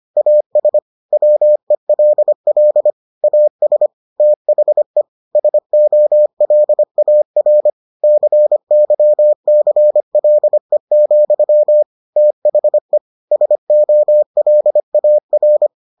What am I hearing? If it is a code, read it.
Morse code, heard as a keyed tone, about 25 wpm, 600 Hz. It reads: AS WELL AS THE SOLAR CYCLE, THE SOLAR